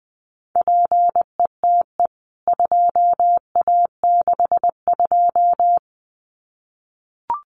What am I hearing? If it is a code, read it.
Morse code, heard as a keyed tone, about 20 wpm, 700 Hz. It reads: PETE 2A62
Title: CWT Exchange 20wpm